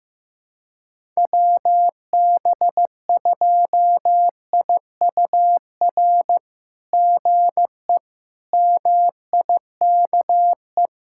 Morse code, heard as a keyed tone, about 15 wpm, 700 Hz. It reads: WB2IUR GE MIKE
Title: SOTA QSO Practice 15wpm